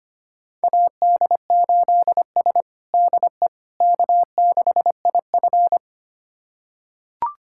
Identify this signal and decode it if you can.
Morse code, heard as a keyed tone, about 25 wpm, 700 Hz. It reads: AD8S DE K6IF
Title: USA Callsign DE Callsign - 4 Characters - 25wpm